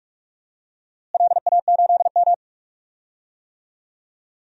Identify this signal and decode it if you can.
Morse code, heard as a keyed tone, about 42 wpm, 700 Hz. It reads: LA8M
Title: DXCC-100 Call Signs - 4 Characters - Repeated 2x - 45wpm